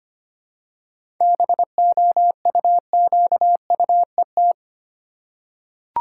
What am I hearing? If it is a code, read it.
Morse code, heard as a keyed tone, about 25 wpm, 700 Hz. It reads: BOUQUET
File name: practice / 25wpm / Difficult Words with Hints - 25wpm